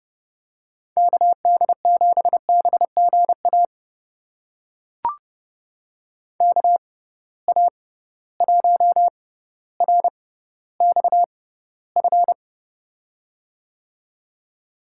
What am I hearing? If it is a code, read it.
Morse code, heard as a keyed tone, about 31 wpm, spaced as for 15 wpm, 700 Hz. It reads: KD7BGA KA1RXF
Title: USA Call Signs - 6 Characters - 6x Character Spacing - 30wpm